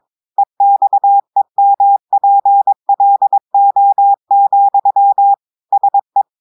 Morse code, heard as a keyed tone, about 22 wpm, 800 Hz. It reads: EXEMPLO, SE
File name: audios / portugais / portugais texte en clair 2 vitesse 22 mots minutes